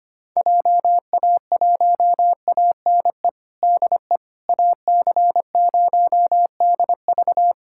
Morse code, heard as a keyed tone, about 25 wpm, 700 Hz. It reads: JA1ANE DE AC0DV
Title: USA Callsign DE Callsign 25wpm